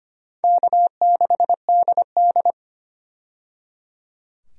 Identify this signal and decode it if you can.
Morse code, heard as a keyed tone, about 25 wpm, 700 Hz. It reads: K6DD
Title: USA Call Signs - 4 Characters - 25wpm